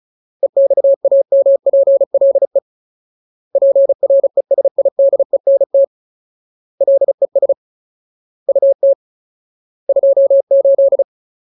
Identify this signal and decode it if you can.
Morse code, heard as a keyed tone, about 34 wpm, 550 Hz. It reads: EXAMPLE PRESIDENT LES UT 28
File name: Ultimate Word List / 35WPM Ultimate